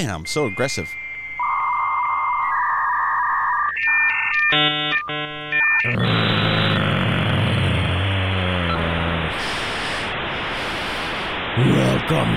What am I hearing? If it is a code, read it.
Dial-up modem handshake, with speech.